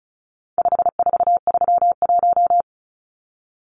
Morse code, heard as a keyed tone, about 34 wpm, 700 Hz. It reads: 5431
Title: Numbers - 4-Digit 35wpm